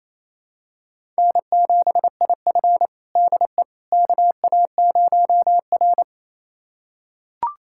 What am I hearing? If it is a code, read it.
Morse code, heard as a keyed tone, about 28 wpm, 700 Hz. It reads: N7IF DE KA0R
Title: USA Callsign DE Callsign - 4 Characters - 28wpm